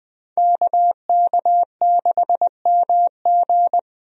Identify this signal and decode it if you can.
Morse code, heard as a keyed tone, about 20 wpm, 700 Hz. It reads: KK6MG
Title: USA Call Signs - 5 Characters - 20wpm